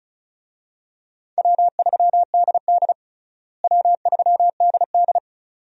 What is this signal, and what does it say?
Morse code, heard as a keyed tone, about 36 wpm, 700 Hz. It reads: W3DD W3DD
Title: USA Call Signs - 4 Characters - Repeated 2x - 35wpm